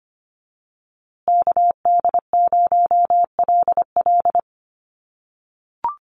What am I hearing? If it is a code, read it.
Morse code, heard as a keyed tone, about 25 wpm, 700 Hz. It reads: KD0LL
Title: USA Call Signs Repeated 3x 25wpm